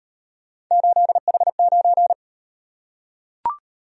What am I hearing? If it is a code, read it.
Morse code, heard as a keyed tone, about 38 wpm, 700 Hz. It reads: 8H9